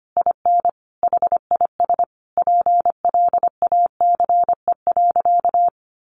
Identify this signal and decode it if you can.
Morse code, heard as a keyed tone, about 25 wpm, 700 Hz. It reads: IN HIS PLACE.
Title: Sentences from Top 1400 Words 25wpm